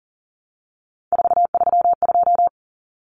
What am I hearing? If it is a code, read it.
Morse code, heard as a keyed tone, about 40 wpm, 700 Hz. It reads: 432
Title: Numbers - 3-Digit 40wpm